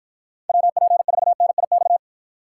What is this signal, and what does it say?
Morse code, heard as a keyed tone, about 52 wpm, 700 Hz. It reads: WP4NIX